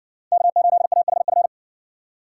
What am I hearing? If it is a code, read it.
Morse code, heard as a keyed tone, about 60 wpm, 700 Hz. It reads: K8AHF